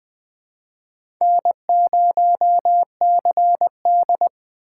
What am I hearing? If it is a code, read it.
Morse code, heard as a keyed tone, about 20 wpm, 700 Hz. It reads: N0CD
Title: USA Call Signs - 4 Characters - 2x Character Spacing - 20wpm